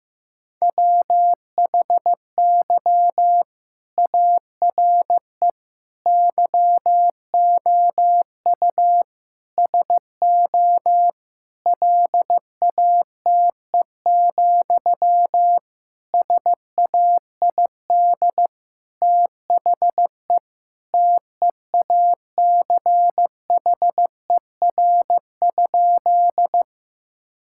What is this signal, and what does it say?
Morse code, heard as a keyed tone, about 15 wpm, 700 Hz. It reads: WHY ARE YOU SO LATE, SAID THE TEACHER?
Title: Sentences from Top 1000 Words - Review Part 6 - 15wpm